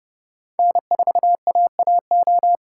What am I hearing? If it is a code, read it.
Morse code, heard as a keyed tone, about 30 wpm, 700 Hz. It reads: N4AAO
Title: USA Call Signs - 5 Characters - 3x Character Spacing - 30wpm